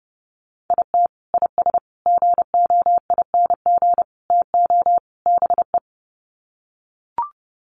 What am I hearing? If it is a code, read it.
Morse code, heard as a keyed tone, about 30 wpm, 700 Hz. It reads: IT IS GOING TO BE
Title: Sets of 5 Words 30wpm